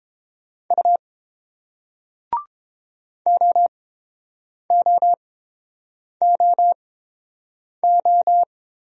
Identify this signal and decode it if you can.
Morse code, heard as a keyed tone, about 30 wpm, spaced as for 7 wpm, 700 Hz. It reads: U OOOO